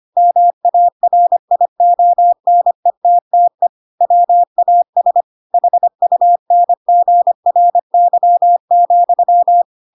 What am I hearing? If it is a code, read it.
Morse code, heard as a keyed tone, about 25 wpm, 700 Hz. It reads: MARIONETTE WAS HUNGRY,
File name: Pinocchio / Pinocchio ch7 25 25 ik1hge